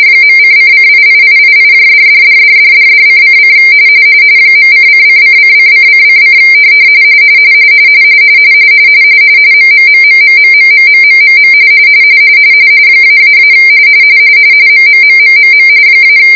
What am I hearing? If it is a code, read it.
RTTY, about 45.45 baud. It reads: .:::88888$$$$$$$::: .:::::::::: MMMMMMMMMMMM --- MMMMMM -